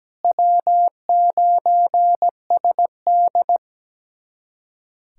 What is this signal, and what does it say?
Morse code, heard as a keyed tone, about 17 wpm, 700 Hz. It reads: W9SD
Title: USA Callsign DE Callsign - 4 Characters - 17wpm